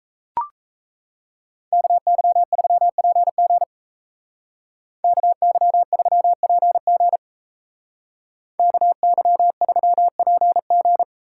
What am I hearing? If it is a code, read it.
Morse code, heard as a keyed tone, about 34 wpm, 700 Hz. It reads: KY3PG KY3PG KY3PG
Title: Speed Racing - USA 2x2 Call Signs 28wpm